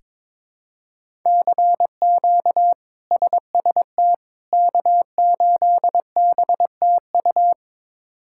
Morse code, heard as a keyed tone, about 22 wpm, 700 Hz. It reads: CQ SST K8BTU
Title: K1USN SST QSO Practice - 22wpm